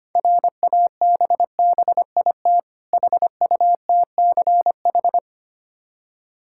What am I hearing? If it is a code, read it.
Morse code, heard as a keyed tone, about 25 wpm, 690 Hz. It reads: RABBIT HUTCH
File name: practice / 25wpm / Difficult Words with Hints - 25wpm